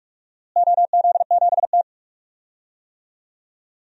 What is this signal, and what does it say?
Morse code, heard as a keyed tone, about 46 wpm, 700 Hz. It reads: OZ7T
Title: DXCC-100 Call Signs - 4 Characters - Repeated 2x - 45wpm